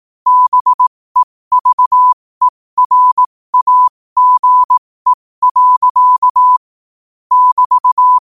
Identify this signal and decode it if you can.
Morse code, heard as a keyed tone, about 18 wpm, spaced as for 15 wpm, 1000 Hz. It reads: BEVERAGE. <BT>
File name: qotdfiles / 15wpm